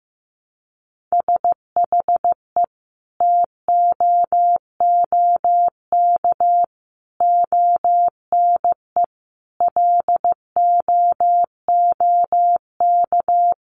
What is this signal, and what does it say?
Morse code, heard as a keyed tone, about 15 wpm, 700 Hz. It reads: SHE TOOK ONE LOOK
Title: Sentences from Top 600 Words 15wpm